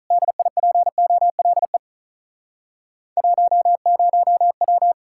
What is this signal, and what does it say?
Morse code, heard as a keyed tone, about 45 wpm, 700 Hz. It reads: DIPOLE 10W